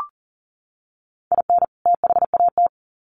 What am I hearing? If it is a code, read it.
Morse code, heard as a keyed tone, about 40 wpm, 700 Hz. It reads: IN THAT
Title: Sets of 2 Words 40wpm